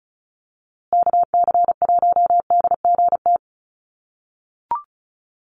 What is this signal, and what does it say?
Morse code, heard as a keyed tone, about 34 wpm, 700 Hz. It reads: KC1DGT
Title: USA Call Signs Repeated 3x 35wpm